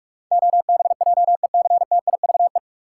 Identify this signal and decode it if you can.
Morse code, heard as a keyed tone, about 45 wpm, 700 Hz. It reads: OBJECTIVE
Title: Sentences from Top 1200 Words - New Word Drill - 45wpm